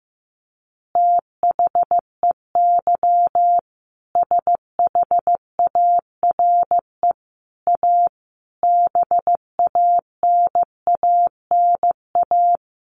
Morse code, heard as a keyed tone, about 15 wpm, 700 Hz. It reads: THEY SHARE A BANANA
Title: Additional Easy Phrases 15wpm